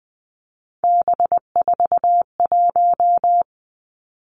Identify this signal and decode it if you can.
Morse code, heard as a keyed tone, about 20 wpm, 700 Hz. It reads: B41
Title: Three Challenging Letter-Number Combinations 20wpm